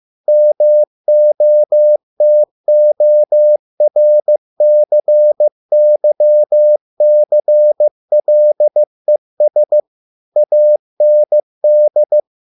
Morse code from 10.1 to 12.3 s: AND

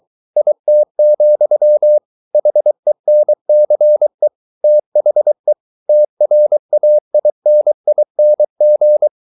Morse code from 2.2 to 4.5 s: HENCE

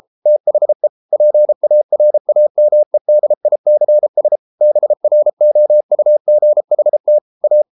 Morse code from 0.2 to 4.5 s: THE PARAMEDICS

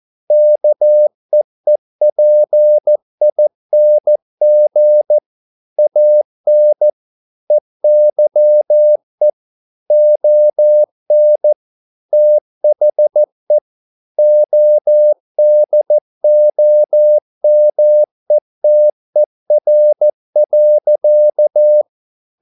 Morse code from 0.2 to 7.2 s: KEEPING AN

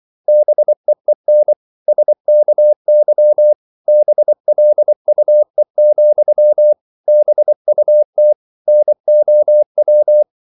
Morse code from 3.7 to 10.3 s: BLUE, BUT NOW

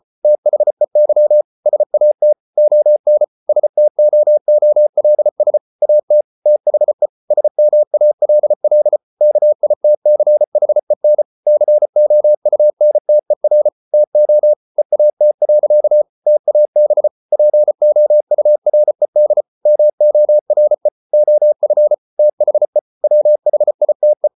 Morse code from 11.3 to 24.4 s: COUNTER TO EAT. TAB POURED MORE OF THE WHITE

